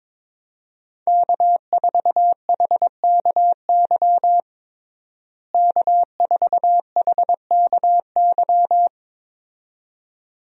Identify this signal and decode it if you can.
Morse code, heard as a keyed tone, about 21 wpm, 700 Hz. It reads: K4HKY K4HKY
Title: USA Call Signs - 5 Characters - Repeated 2x - 22wpm